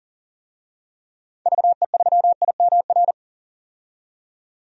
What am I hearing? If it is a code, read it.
Morse code, heard as a keyed tone, about 40 wpm, 700 Hz. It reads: VE3IMR
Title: DXCC-100 Call Signs - 6 Characters - 40wpm